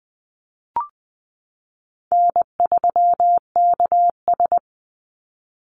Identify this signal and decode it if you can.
Morse code, heard as a keyed tone, about 19 wpm, 700 Hz. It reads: N3KS
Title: Call Signs Repeated 3x - 20wpm